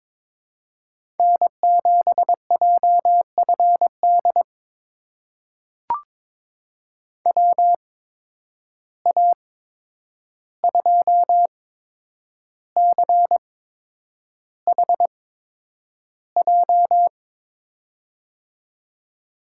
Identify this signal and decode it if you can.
Morse code, heard as a keyed tone, about 21 wpm, spaced as for 9 wpm, 700 Hz. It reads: N7JFD WA2CHJ
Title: USA Call Signs - 8x Character Spacing - 22wpm